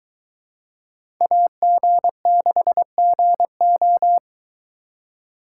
Morse code, heard as a keyed tone, about 22 wpm, 700 Hz. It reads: AG6GO